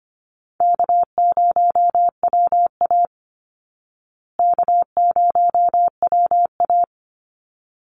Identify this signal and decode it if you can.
Morse code, heard as a keyed tone, about 25 wpm, 700 Hz. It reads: K0WA K0WA
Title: Easy Call Signs Repeated 2x - 25wpm